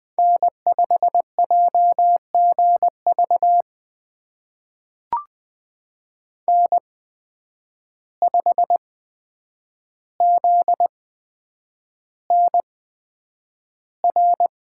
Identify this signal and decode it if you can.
Morse code, heard as a keyed tone, about 20 wpm, spaced as for 9 wpm, 700 Hz. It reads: N5JGV N5ZNR
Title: USA Call Signs - 5 Characters - 8x Character Spacing - 20wpm